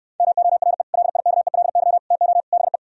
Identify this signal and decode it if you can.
Morse code, heard as a keyed tone, about 68 wpm, 700 Hz. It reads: MORE LIKELY TO BE